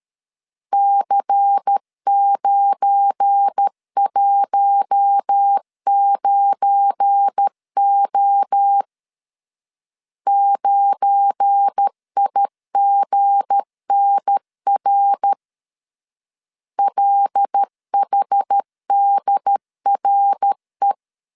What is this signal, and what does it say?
Morse code, heard as a keyed tone, about 13 wpm, 800 Hz. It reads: C919O 9IGNR LHDRE
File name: LEZIONI / 7 GF9Z / 7 GF9 12wpm 2319